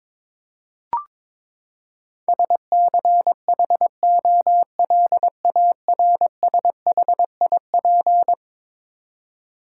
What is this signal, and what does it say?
Morse code, heard as a keyed tone, about 22 wpm, 700 Hz. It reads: SCHOLARSHIP
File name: practice / 22wpm / Eleven Letter Words - 22wpm